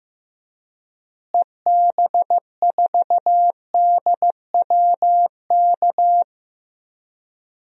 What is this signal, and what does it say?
Morse code, heard as a keyed tone, about 15 wpm, 700 Hz. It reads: EB4DWK
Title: DXCC-100 Call Signs - 6 Characters - Repeated 3x - 15wpm